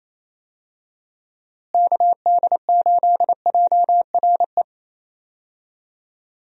Morse code, heard as a keyed tone, about 28 wpm, 700 Hz. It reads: KD8JRE
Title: USA Call Signs - 6 Characters - Repeated 3x - 28wpm